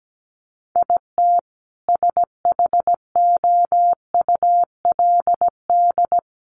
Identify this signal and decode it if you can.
Morse code, heard as a keyed tone, about 17 wpm, 700 Hz. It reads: IT SHOULD
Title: Sets of 5 Words 17wpm